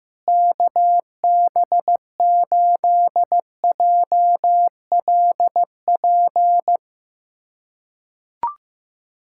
Morse code, heard as a keyed tone, about 15 wpm, 700 Hz. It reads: KB8JLP
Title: USA Callsign DE Callsign - 6 Characters - 15wpm